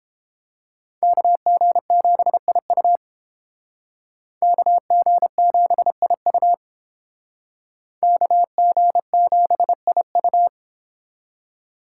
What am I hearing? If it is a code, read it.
Morse code, heard as a keyed tone, about 30 wpm, 700 Hz. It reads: KG7IU KG7IU KG7IU